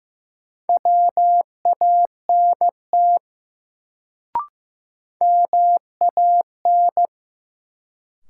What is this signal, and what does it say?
Morse code, heard as a keyed tone, about 15 wpm, 700 Hz. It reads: WANT MAN
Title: Sets of 2 Words - New Word Drill - 15wpm